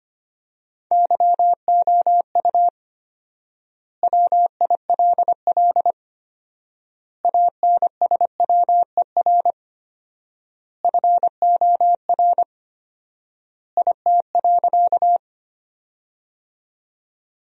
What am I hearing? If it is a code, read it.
Morse code, heard as a keyed tone, about 25 wpm, 700 Hz. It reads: YOU WILL ANSWER FOR IT.
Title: 4x Word-Spacing - Sentences from Top 300 Words - 25wpm